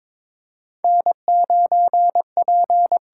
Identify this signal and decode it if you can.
Morse code, heard as a keyed tone, about 22 wpm, 700 Hz. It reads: N9P
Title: USA Call Signs - 4 Characters - 8x Character Spacing - 22wpm